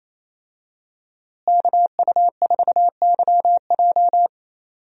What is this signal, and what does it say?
Morse code, heard as a keyed tone, about 28 wpm, 700 Hz. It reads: KU4YJ
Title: USA Call Signs - 5 Characters - Repeated 2x - 28wpm